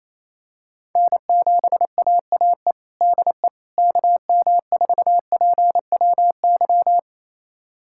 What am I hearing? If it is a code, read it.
Morse code, heard as a keyed tone, about 28 wpm, 700 Hz. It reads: N7AAE DE KM4PWY
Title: USA Callsign DE Callsign 28wpm